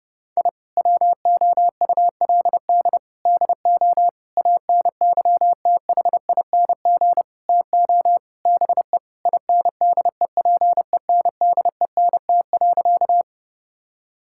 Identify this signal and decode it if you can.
Morse code, heard as a keyed tone, about 30 wpm, 700 Hz. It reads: I WOULD DO ANYTHING TO BE INDEPENDENT.
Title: Sentences from Top 1000 Words - Review Part 6 - 30wpm